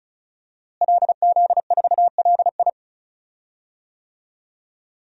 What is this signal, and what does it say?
Morse code, heard as a keyed tone, about 34 wpm, 700 Hz. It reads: LZ4LI
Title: DXCC-100 Call Signs - 5 Characters - Repeated 3x - 35wpm